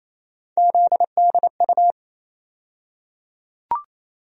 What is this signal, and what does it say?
Morse code, heard as a keyed tone, about 28 wpm, 700 Hz. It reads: ZDU
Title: USA Call Sign Postfixes - 3 Letters 28wpm